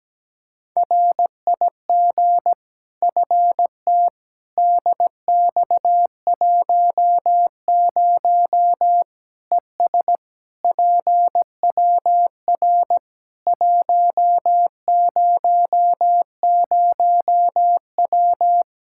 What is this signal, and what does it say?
Morse code, heard as a keyed tone, about 17 wpm, 700 Hz. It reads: RIG FT DX10 ES PWR 100W
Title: Practice QSO - Exchange 2 - Info - 17wpm